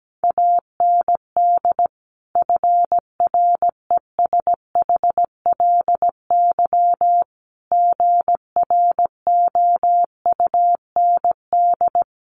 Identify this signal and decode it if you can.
Morse code, heard as a keyed tone, about 17 wpm, 690 Hz. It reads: AND FRESHLY GROUND
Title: Sets of 5 Words Encore 17wpm